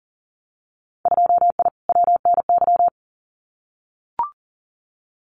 Morse code, heard as a keyed tone, about 42 wpm, 700 Hz. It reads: 2I WNY
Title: Winter Field Day Exchanges Encore 40wpm